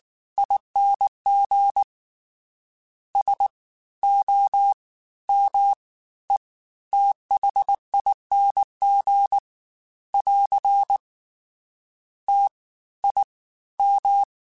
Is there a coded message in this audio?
Yes: Morse code from 0.4 to 14.2 s, heard as a keyed tone, about 19 wpm, spaced as for 14 wpm, 800 Hz. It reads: ING SOMETHING <AR> TIM